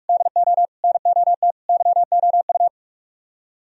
Morse code, heard as a keyed tone, about 50 wpm, 700 Hz. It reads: DO NOT YOU